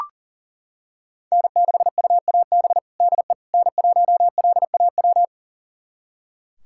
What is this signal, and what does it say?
Morse code, heard as a keyed tone, about 40 wpm, 700 Hz. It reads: N6UAB DE N1LAW
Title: USA Callsign DE Callsign - 5 Characters - 40wpm